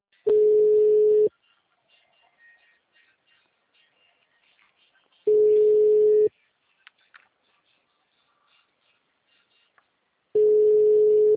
Ringback tone (Europe, 425 Hz).
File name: out- -2011- - - .54434